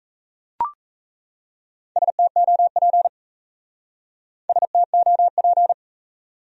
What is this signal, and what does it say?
Morse code, heard as a keyed tone, about 33 wpm, 700 Hz. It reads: STOP STOP
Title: Speed-Racing Top 300 Words 28wpm